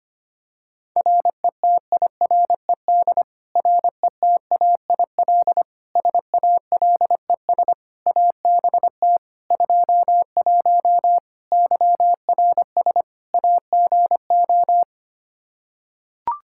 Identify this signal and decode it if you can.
Morse code, heard as a keyed tone, about 25 wpm, 700 Hz. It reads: RETIRED RETAIL SALES ABT 21 YRS AGO
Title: Practice QSO - Exchange 3 - More Info - 25wpm